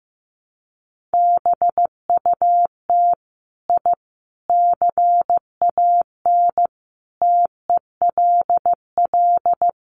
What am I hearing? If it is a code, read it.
Morse code, heard as a keyed tone, about 15 wpm, 700 Hz. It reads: BUT I CAN TELL
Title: Sets of 5 Words Encore 15wpm